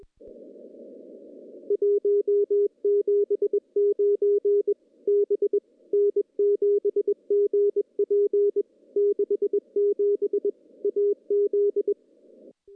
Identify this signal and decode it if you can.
Morse code, heard as a keyed tone, about 21 wpm, 410 Hz. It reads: 179 B N7GP 67 AZ
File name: contests / CW-SS-k5tr- -1500